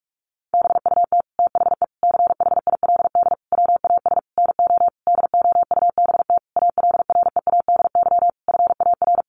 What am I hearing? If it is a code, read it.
Morse code, heard as a keyed tone, about 44 wpm, 700 Hz. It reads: BUT THE CHILD WAS NO DOUBT ALREADY FAR
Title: Sentences from Top 1500 Words 45wpm